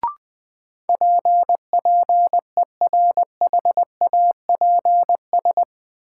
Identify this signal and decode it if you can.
Morse code, heard as a keyed tone, about 21 wpm, 700 Hz. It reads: P PERHAPS
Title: Sentences from Top 1100 Words - New Word Drill - 20wpm